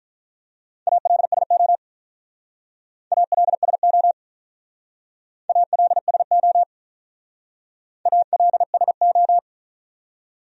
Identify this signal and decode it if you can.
Morse code, heard as a keyed tone, about 43 wpm, 700 Hz. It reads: ALSO ALSO ALSO ALSO
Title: Speed-Racing Top 500 Words 35wpm